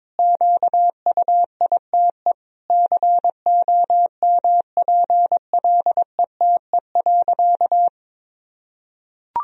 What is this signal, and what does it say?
Morse code, heard as a keyed tone, about 21 wpm, 700 Hz. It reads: QUITE COMPLETE.
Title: Sentences from Top 1000 Words - Review Part 2 - 22wpm